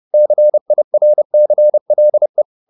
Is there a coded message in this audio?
Yes: Morse code from 0.1 to 2.4 s, heard as a keyed tone, about 30 wpm, 600 Hz. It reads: CIRCLE